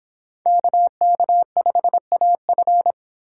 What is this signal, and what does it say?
Morse code, heard as a keyed tone, about 25 wpm, 700 Hz. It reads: KK5AF